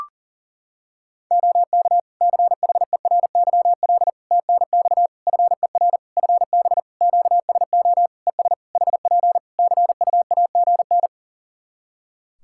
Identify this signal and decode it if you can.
Morse code, heard as a keyed tone, about 40 wpm, 700 Hz. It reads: OK CHERYL TNX FER FB QSO ES HP CUAGN
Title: Practice QSO - Exchange 4 - Ending - 40wpm